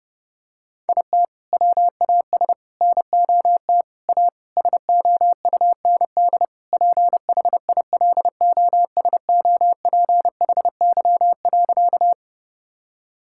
Morse code, heard as a keyed tone, about 30 wpm, 700 Hz. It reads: IT WAS NOT A SOUND PHILOSOPHY.
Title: Sentences from Top 1000 Words - Review Part 6 - 30wpm